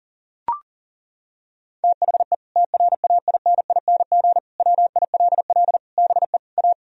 Morse code, heard as a keyed tone, about 40 wpm, 700 Hz. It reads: THE TRAINING WILL BE A
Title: Sentences from Top 1000 Words - Review Part 4 - 40wpm